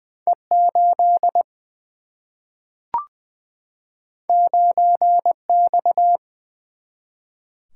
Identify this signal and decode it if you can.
Morse code, heard as a keyed tone, about 20 wpm, 700 Hz. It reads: E 8 9 X
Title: Single-Letter Single-Number 20wpm